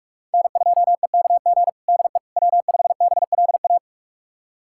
Morse code, heard as a keyed tone, about 48 wpm, 700 Hz. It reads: N2EKG DE W5BLA
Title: USA Callsign DE Callsign - 5 Characters - 45wpm